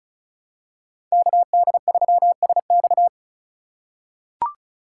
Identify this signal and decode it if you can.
Morse code, heard as a keyed tone, about 36 wpm, 700 Hz. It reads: KD3SX
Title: USA Call Signs - 5 Characters - 2x Character Spacing - 35wpm